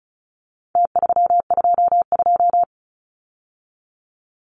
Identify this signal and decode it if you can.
Morse code, heard as a keyed tone, about 34 wpm, 700 Hz. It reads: T322
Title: Numbers - Contest Style - 4-Digit 35wpm